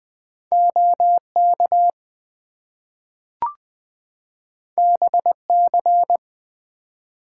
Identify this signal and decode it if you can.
Morse code, heard as a keyed tone, about 20 wpm, 700 Hz. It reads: OK BC
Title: US States and Canadian Provinces 20wpm